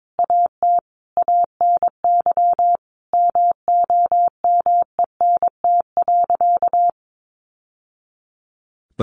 Morse code, heard as a keyed tone, about 22 wpm, 700 Hz. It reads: AT ANY MOMENT.
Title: Sentences from Top 1100 Words 22wpm